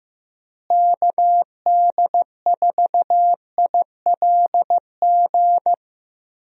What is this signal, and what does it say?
Morse code, heard as a keyed tone, about 16 wpm, 700 Hz. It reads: KD4ILG
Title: USA Call Signs - 6 Characters - Repeated 2x - 15wpm